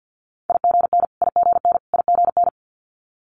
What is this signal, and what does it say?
Morse code, heard as a keyed tone, about 50 wpm, 700 Hz. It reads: IGN IGN IGN
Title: Multi-letter Phonemes Repeated 3x 50wpm